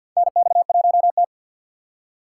Morse code, heard as a keyed tone, about 52 wpm, 700 Hz. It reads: NX1T
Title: USA Call Signs - 4 Characters - 3x Character Spacing - 50wpm